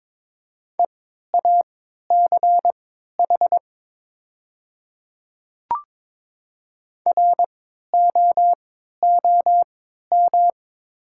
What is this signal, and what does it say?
Morse code, heard as a keyed tone, about 22 wpm, spaced as for 13 wpm, 700 Hz. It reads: EACH ROOM
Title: 3x Character Spacing - Top 300 Words - 22wpm